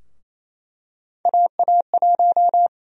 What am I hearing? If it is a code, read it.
Morse code, heard as a keyed tone, about 28 wpm, 700 Hz. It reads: AA1
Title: USA Call Sign Prefixes - Repeated 2x - 28wpm